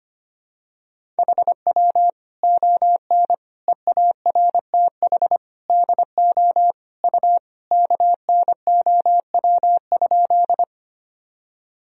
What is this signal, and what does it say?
Morse code, heard as a keyed tone, about 25 wpm, 700 Hz. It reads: HW ON EARTH DO U KNOW?
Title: Sentences from Top 200 Words and Abbreviations 25wpm